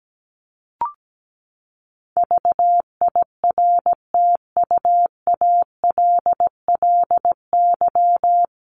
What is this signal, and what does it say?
Morse code, heard as a keyed tone, about 17 wpm, 700 Hz. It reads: VIRTUALLY
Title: Top 1501-1750 Words 17wpm